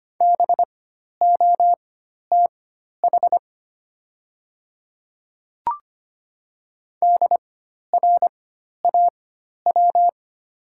Morse code, heard as a keyed tone, about 25 wpm, spaced as for 12 wpm, 700 Hz. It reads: BOTH DRAW
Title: 4x Character Spacing - Top 300 Words - 25wpm